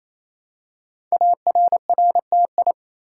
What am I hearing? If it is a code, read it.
Morse code, heard as a keyed tone, about 29 wpm, 700 Hz. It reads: ARRTI
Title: Code Groups - 5 Letters - 28wpm